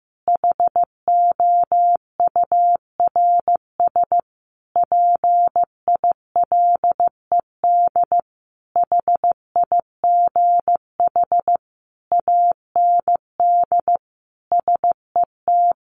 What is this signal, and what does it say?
Morse code, heard as a keyed tone, about 15 wpm, 700 Hz. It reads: HOURS PILED HIGH AND SET
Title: A Sound Of Thunder 15wpm